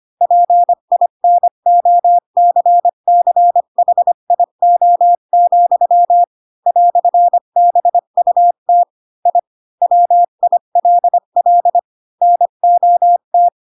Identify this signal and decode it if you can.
Morse code, heard as a keyed tone, about 25 wpm, 700 Hz. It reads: PINOCCHIO, "BUT I WILL NOT
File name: Pinocchio / Pinocchio ch7 20 25 ik1hge